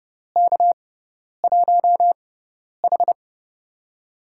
Morse code, heard as a keyed tone, about 30 wpm, spaced as for 10 wpm, 700 Hz. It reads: K1H
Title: USA Call Signs - 4 Characters - 6x Character Spacing - 30wpm